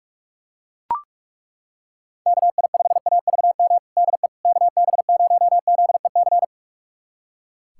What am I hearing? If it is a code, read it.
Morse code, heard as a keyed tone, about 41 wpm, 700 Hz. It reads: KI5AVM DE KB0ZEC
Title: USA Callsign DE Callsign - 6 Characters - 45wpm